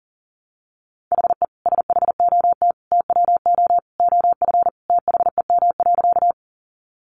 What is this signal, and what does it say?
Morse code, heard as a keyed tone, about 40 wpm, 700 Hz. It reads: HE SHOT TWO OF THEM.
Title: Sentences from Top 1100 Words 40wpm